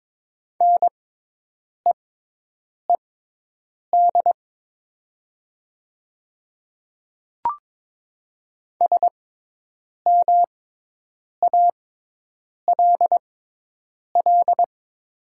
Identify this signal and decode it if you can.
Morse code, heard as a keyed tone, about 22 wpm, spaced as for 8 wpm, 700 Hz. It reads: NEED SMALL
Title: 6x Character Spacing - Top 200 Words - 22wpm